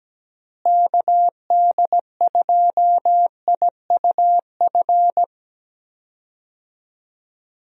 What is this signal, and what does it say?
Morse code, heard as a keyed tone, about 17 wpm, 700 Hz. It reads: KD2IUF
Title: USA Call Signs - 6 Characters - Repeated 2x - 17wpm